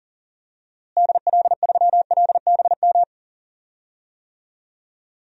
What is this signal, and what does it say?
Morse code, heard as a keyed tone, about 37 wpm, 700 Hz. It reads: DL3LBM
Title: DXCC-100 Call Signs - 6 Characters - Repeated 3x - 40wpm